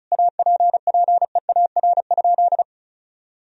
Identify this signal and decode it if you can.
Morse code, heard as a keyed tone, about 36 wpm, 700 Hz. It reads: APPEAR?
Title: Sentences from Top 1000 Words - Review Part 2 - 35wpm